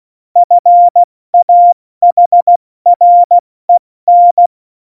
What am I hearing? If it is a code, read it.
Morse code, heard as a keyed tone, about 16 wpm, spaced as for 14 wpm, 700 Hz. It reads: FAHREN
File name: Morsekurs / Witze / Witz 11. Februar 2020, 70 F80